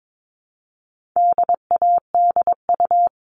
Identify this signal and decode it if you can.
Morse code, heard as a keyed tone, about 22 wpm, 700 Hz. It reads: DADU
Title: All Cut Numbers - 4-Digit 22wpm